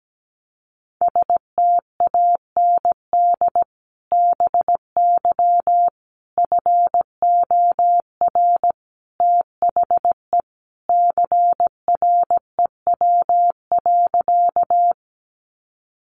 Morse code, heard as a keyed tone, about 17 wpm, 700 Hz. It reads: STAND BY FOR THE CREW.
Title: Sentences from Top 1500 Words 17wpm